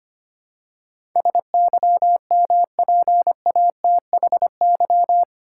Morse code, heard as a keyed tone, about 25 wpm, 700 Hz. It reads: SYMPATHY
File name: Eight Letter Words - 25wpm